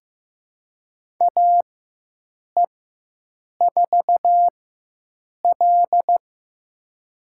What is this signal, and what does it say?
Morse code, heard as a keyed tone, about 15 wpm, spaced as for 7 wpm, 700 Hz. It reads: AE4L
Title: USA Call Signs - 4 Characters - 4x Character Spacing - 15wpm